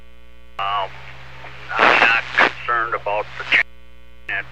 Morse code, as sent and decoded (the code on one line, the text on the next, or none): none